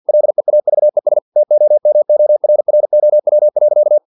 Morse code, heard as a keyed tone, about 50 wpm, 600 Hz. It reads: LEAVES TOMORROW.
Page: Never 49wpm